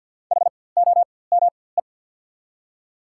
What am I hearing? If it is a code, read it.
Morse code, heard as a keyed tone, about 53 wpm, spaced as for 24 wpm, 690 Hz. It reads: HOME
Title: 4x Character Spacing - Top 500 Words - 50wpm